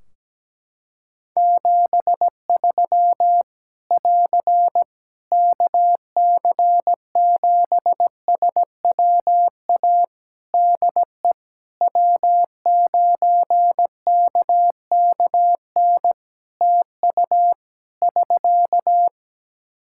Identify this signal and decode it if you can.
Morse code, heard as a keyed tone, about 17 wpm, 700 Hz. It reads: 73 <AR> KC7SWA DE W9KKN TU <SK>
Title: Practice QSO - Exchange 4 - Ending - 17wpm